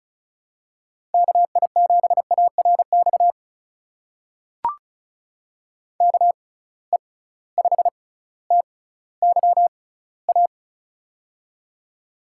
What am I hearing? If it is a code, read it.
Morse code, heard as a keyed tone, about 36 wpm, spaced as for 18 wpm, 700 Hz. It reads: KI7ARX KE5TYA
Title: USA Call Signs - 6 Characters - 6x Character Spacing - 35wpm